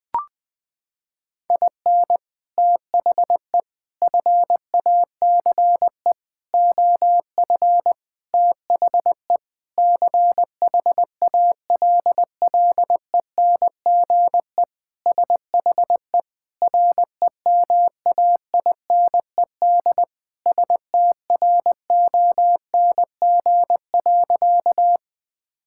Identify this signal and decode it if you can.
Morse code, heard as a keyed tone, about 20 wpm, 700 Hz. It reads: IN THE FACE OF THE CHALLENGE SHE REMAINED STRONG.
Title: Sentences with 5-Word Phrases - 20wpm